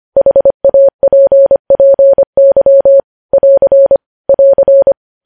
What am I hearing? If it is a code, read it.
Morse code, heard as a keyed tone, about 25 wpm, 550 Hz. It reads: HAPPY <AR> <AR>
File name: podcast / News headlines-1x-25x25